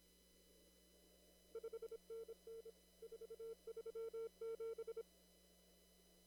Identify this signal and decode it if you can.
Morse code, heard as a keyed tone, about 26 wpm, 440 Hz. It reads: 5NN 437
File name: CW / kt5j wpx cw-op-k5pi